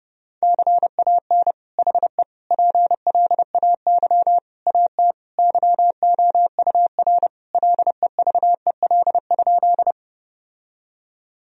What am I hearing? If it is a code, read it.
Morse code, heard as a keyed tone, about 32 wpm, 700 Hz. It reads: CAN HE PLAY AT YOUR LEVEL?
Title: Sentences from Top 1000 Words - Review Part 4 - 30wpm